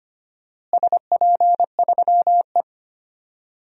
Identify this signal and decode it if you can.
Morse code, heard as a keyed tone, about 25 wpm, 700 Hz. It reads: SP3E
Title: DXCC-100 Call Signs - 4 Characters - Repeated 2x - 25wpm